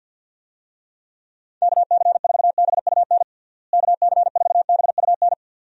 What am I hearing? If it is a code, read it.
Morse code, heard as a keyed tone, about 53 wpm, 700 Hz. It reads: KC4BUN KC4BUN
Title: USA Call Signs - 6 Characters - Repeated 2x - 50wpm